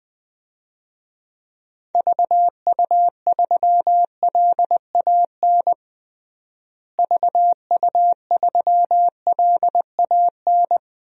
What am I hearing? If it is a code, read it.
Morse code, heard as a keyed tone, about 20 wpm, 700 Hz. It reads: VU3LAN VU3LAN
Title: DXCC-100 Call Signs - 6 Characters - Repeated 3x - 20wpm